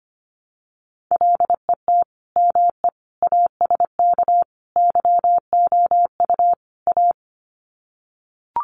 Morse code, heard as a keyed tone, about 25 wpm, 700 Hz. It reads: LET ME ASK YOU A
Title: Sets of 5 Words Encore 25wpm